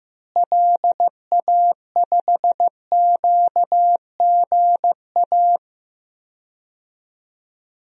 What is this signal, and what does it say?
Morse code, heard as a keyed tone, about 15 wpm, 700 Hz. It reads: LA5QGA
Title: DXCC-100 Call Signs - 6 Characters - Repeated 2x - 15wpm